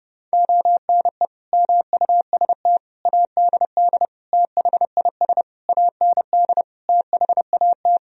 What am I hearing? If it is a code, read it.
Morse code, heard as a keyed tone, about 30 wpm, 700 Hz. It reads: ONE MUST ADD THIS AND THAT
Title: Sentences from Top 1000 Words - Review Part 1 - 30wpm